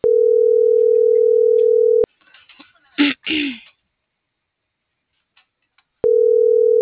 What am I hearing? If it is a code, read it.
Ringback tone (North America), 440 + 480 Hz.